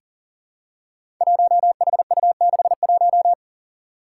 Morse code, heard as a keyed tone, about 40 wpm, 700 Hz. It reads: 1HU61
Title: Five Challenging Letter-Number Combinations 40wpm